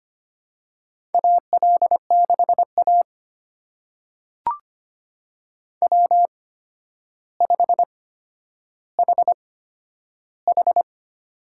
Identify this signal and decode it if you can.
Morse code, heard as a keyed tone, about 25 wpm, spaced as for 11 wpm, 700 Hz. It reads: AL6A W5HH
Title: USA Call Signs - 4 Characters - 8x Character Spacing - 25wpm